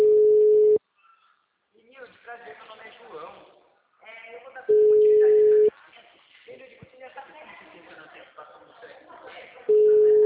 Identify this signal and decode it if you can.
Ringback tone (Europe, 425 Hz), with speech.